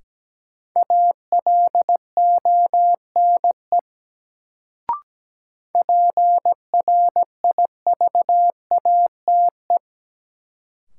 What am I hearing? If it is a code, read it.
Morse code, heard as a keyed tone, about 17 wpm, 700 Hz. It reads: ALONE PRIVATE
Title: Sentences from Top 1000 Words - New Word Drill - 17wpm